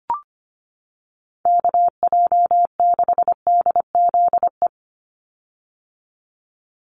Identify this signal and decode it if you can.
Morse code, heard as a keyed tone, about 25 wpm, 700 Hz. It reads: KJ6DZE
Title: USA Call Signs 25wpm